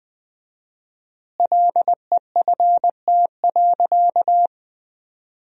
Morse code, heard as a keyed tone, about 19 wpm, 700 Hz. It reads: LEFT.
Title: 4x Word-Spacing - Sentences from Top 300 Words - 20wpm